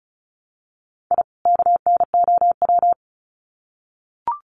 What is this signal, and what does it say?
Morse code, heard as a keyed tone, about 34 wpm, 700 Hz. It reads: I KNOW
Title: Sets of 2 Words 35wpm